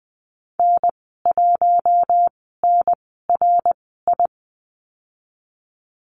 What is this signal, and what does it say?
Morse code, heard as a keyed tone, about 20 wpm, spaced as for 15 wpm, 700 Hz. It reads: N1NRI
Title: 055 Lesson 10g - Letter r - Call Signs 20wpm